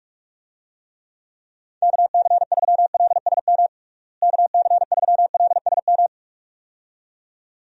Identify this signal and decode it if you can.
Morse code, heard as a keyed tone, about 42 wpm, 700 Hz. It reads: KC3LSM KC3LSM
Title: USA Call Signs - 6 Characters - Repeated 2x - 45wpm